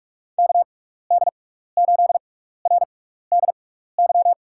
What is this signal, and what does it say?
Morse code, heard as a keyed tone, about 44 wpm, spaced as for 15 wpm, 700 Hz. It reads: KD8RDY
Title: USA Call Signs - 6 Characters - 6x Character Spacing - 45wpm